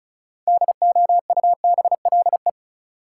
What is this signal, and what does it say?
Morse code, heard as a keyed tone, about 36 wpm, 700 Hz. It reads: DOUBLE
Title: Sentences from Top 1400 Words - New Word Drill - 35wpm